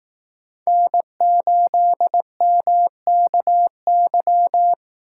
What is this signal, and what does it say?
Morse code, heard as a keyed tone, about 19 wpm, 700 Hz. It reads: N8MKY
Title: Speed Racing - USA 1x3 Call Signs 15wpm